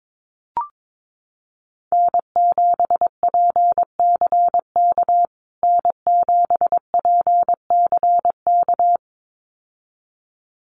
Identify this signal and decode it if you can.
Morse code, heard as a keyed tone, about 22 wpm, 700 Hz. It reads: N7PCK N7PCK
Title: USA Call Signs Repeated 2x 22wpm